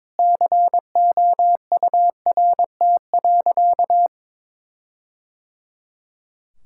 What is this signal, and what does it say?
Morse code, heard as a keyed tone, about 21 wpm, 700 Hz. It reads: COURT.
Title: Sentences from Top 1600 Words 22wpm